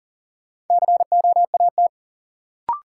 Morse code, heard as a keyed tone, about 40 wpm, 700 Hz. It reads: COAT
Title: Sentences from Top 1600 Words - New Word Drill Alt - 40wpm